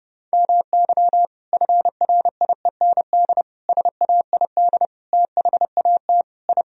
Morse code, heard as a keyed tone, about 32 wpm, 700 Hz. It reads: MY FRIEND SAID THAT I
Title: Sentences from Top 1000 Words - Review Part 4 - 30wpm